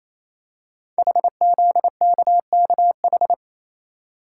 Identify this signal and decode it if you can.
Morse code, heard as a keyed tone, about 29 wpm, 700 Hz. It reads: HZKKH
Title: Code Groups - 5 Letters - 28wpm